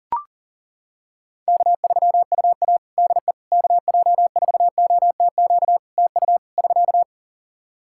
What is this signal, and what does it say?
Morse code, heard as a keyed tone, about 40 wpm, 700 Hz. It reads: K3UA DE KJ4OTQ TU <SK>
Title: Practice QSO - Exchange 4 - Ending - 40wpm